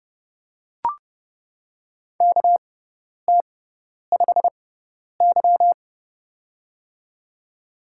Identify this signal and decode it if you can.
Morse code, heard as a keyed tone, about 29 wpm, spaced as for 10 wpm, 700 Hz. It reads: KT5Y
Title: USA Call Signs - 4 Characters - 6x Character Spacing - 30wpm